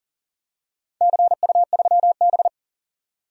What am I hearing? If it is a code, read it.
Morse code, heard as a keyed tone, about 40 wpm, 700 Hz. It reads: CU3B
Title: DXCC-100 Call Signs - 4 Characters - 40wpm